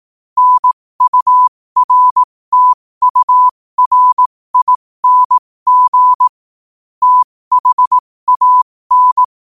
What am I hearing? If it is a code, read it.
Morse code, heard as a keyed tone, about 18 wpm, spaced as for 15 wpm, 1000 Hz. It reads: NURTURING THAN
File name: qotdfiles / 15wpm